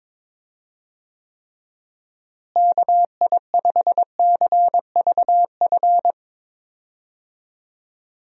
Morse code, heard as a keyed tone, about 22 wpm, 700 Hz. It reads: KI5CVF
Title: USA Call Signs - 6 Characters - Repeated 2x - 22wpm